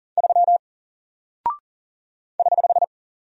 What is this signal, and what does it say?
Morse code, heard as a keyed tone, about 40 wpm, 700 Hz. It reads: 3 <HH>
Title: Single Character (Letter:Number:Punctuation:Prosign) Alt 40wpm